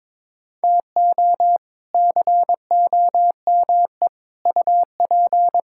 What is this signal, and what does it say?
Morse code, heard as a keyed tone, about 22 wpm, 700 Hz. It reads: TO COME UP
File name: practice / 22wpm / Sets of 7 Words - 22wpm